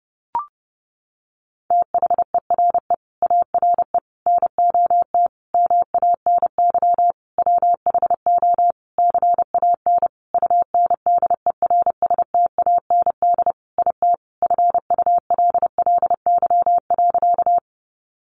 Morse code, heard as a keyed tone, about 30 wpm, 700 Hz. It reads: THERE ARE NOT MANY WHO CAN UNDERSTAND IT FULLY.
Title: Sentences from Top 1300 Words 30wpm